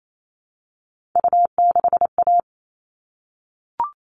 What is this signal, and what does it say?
Morse code, heard as a keyed tone, about 28 wpm, 700 Hz. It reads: U6A
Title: All Cut Numbers - 3-Digit 28wpm